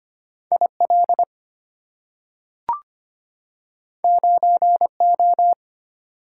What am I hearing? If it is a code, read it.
Morse code, heard as a keyed tone, about 25 wpm, 700 Hz. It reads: IL 9O
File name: Two Letter-Number Combinations - 25wpm